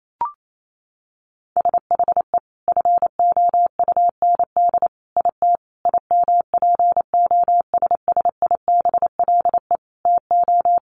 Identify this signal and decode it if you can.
Morse code, heard as a keyed tone, about 28 wpm, 700 Hz. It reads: SHE FOUND IT IMPOSSIBLE TO
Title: Sentences from Top 1100 Words 28wpm